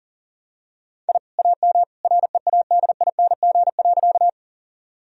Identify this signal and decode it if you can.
Morse code, heard as a keyed tone, about 40 wpm, 700 Hz. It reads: I AM READING.
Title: Sentences from Top 1000 Words - Review Part 3 - 40wpm